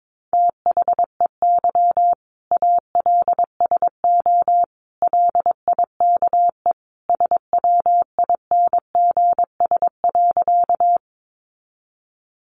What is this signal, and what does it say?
Morse code, heard as a keyed tone, about 22 wpm, 700 Hz. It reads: THEY ALSO LIKE SWINGS.
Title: Easy Phrases 22wpm